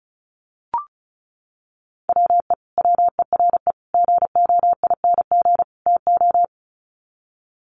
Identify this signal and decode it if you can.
Morse code, heard as a keyed tone, about 34 wpm, 700 Hz. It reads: WE WERE GOING TO
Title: Sets of 4 Words 35wpm